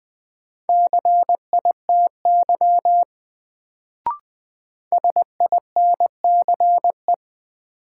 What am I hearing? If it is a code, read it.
Morse code, heard as a keyed tone, about 20 wpm, 700 Hz. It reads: CITY SINCE
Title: Sentences from Top 300 Words - New Word Drill - 20wpm